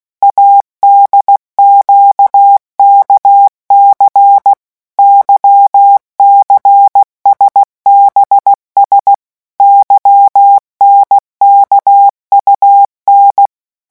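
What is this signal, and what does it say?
Morse code, heard as a keyed tone, about 16 wpm, 800 Hz. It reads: ADQKC YCSBS YNKUN
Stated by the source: Morse letter-group revision drill at 15 words per minute.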